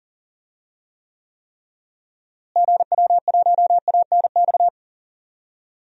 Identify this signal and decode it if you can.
Morse code, heard as a keyed tone, about 40 wpm, 700 Hz. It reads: GW1ANX
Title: DXCC-100 Call Signs - 6 Characters - Repeated 3x - 40wpm